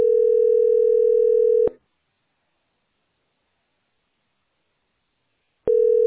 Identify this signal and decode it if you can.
Ringback tone (North America), 440 + 480 Hz.